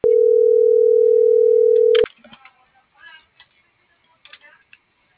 Ringback tone (North America), 440 + 480 Hz.
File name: CENTRAL MABE / out xquenda.trinidad - .67349